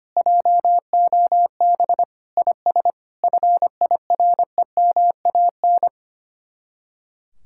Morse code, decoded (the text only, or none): JOB IS FIREMAN